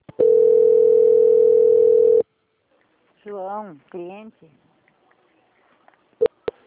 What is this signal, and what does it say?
Ringback tone (North America), 440 + 480 Hz, with speech.
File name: out- - - - - .12078